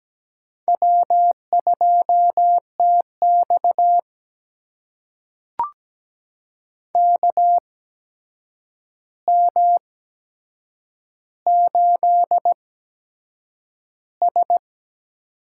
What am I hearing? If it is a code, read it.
Morse code, heard as a keyed tone, about 17 wpm, spaced as for 7 wpm, 700 Hz. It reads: W2TX KM8S
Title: USA Call Signs - 4 Characters - 8x Character Spacing - 17wpm